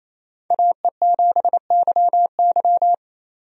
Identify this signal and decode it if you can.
Morse code, heard as a keyed tone, about 28 wpm, 700 Hz. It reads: AE7YY
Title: USA Call Signs - 5 Characters - 28wpm